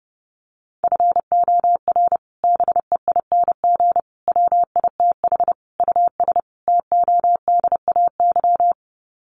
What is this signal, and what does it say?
Morse code, heard as a keyed tone, about 30 wpm, 700 Hz. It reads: FOR BEING WITH US TODAY
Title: Sets of 5 Words Encore2 30wpm